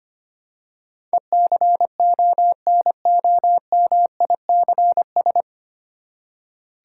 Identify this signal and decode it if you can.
Morse code, heard as a keyed tone, about 25 wpm, 700 Hz. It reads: ECONOMICS
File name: Nine Letter Words - 25wpm